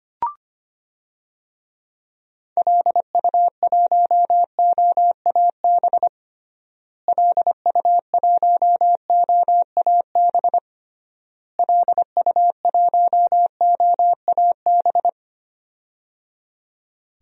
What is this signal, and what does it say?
Morse code, heard as a keyed tone, about 25 wpm, 700 Hz. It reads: LU1OAB LU1OAB LU1OAB
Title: DXCC-100 Call Signs - 6 Characters - Repeated 3x - 25wpm